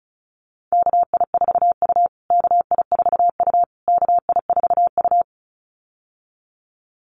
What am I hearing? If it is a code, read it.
Morse code, heard as a keyed tone, about 34 wpm, 700 Hz. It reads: KI4U KI4U KI4U
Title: USA Call Signs Repeated 3x 35wpm